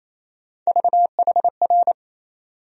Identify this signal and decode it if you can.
Morse code, heard as a keyed tone, about 28 wpm, 700 Hz. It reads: VHR
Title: USA Call Sign Postfixes - 3 Letters 28wpm